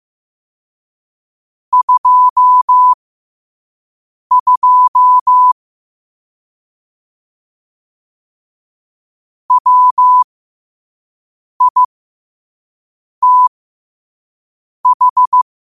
Morse code, heard as a keyed tone, about 15 wpm, spaced as for 5 wpm, 1000 Hz. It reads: 22 WITH
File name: qotdfiles / 5wpm